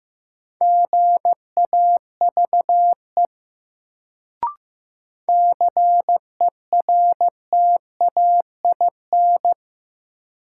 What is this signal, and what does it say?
Morse code, heard as a keyed tone, about 15 wpm, 700 Hz. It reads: GAVE CERTAIN
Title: Sentences from Top 500 Words - New Word Drill - 15wpm